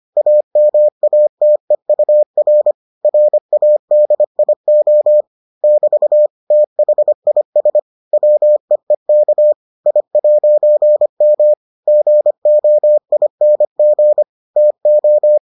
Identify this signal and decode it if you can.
Morse code, heard as a keyed tone, about 25 wpm, 600 Hz. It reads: AMATEUR RADIO <BT> THIS WEEK I'M GOING TO